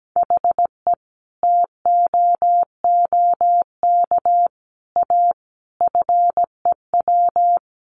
Morse code, heard as a keyed tone, about 17 wpm, 700 Hz. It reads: HE TOOK A FEW
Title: A Sound Of Thunder 17wpm